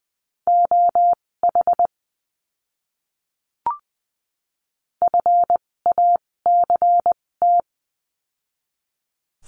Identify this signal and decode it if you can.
Morse code, heard as a keyed tone, about 20 wpm, spaced as for 16 wpm, 700 Hz. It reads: OH FACT
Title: 256 Lesson 42a - Words 12-20wpm